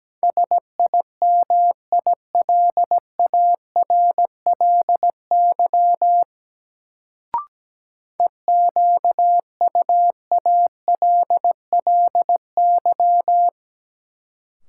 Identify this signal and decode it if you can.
Morse code, heard as a keyed tone, about 17 wpm, 700 Hz. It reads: SIMILARLY EQUALLY
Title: Sentences from Top 1400 Words - New Word Drill Alt - 17wpm